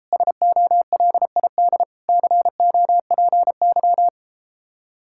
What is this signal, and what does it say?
Morse code, heard as a keyed tone, about 33 wpm, 700 Hz. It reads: SOLID COPY